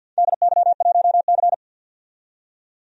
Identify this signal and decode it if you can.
Morse code, heard as a keyed tone, about 49 wpm, 700 Hz. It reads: DY1C
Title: DXCC-100 Call Signs - 4 Characters - 50wpm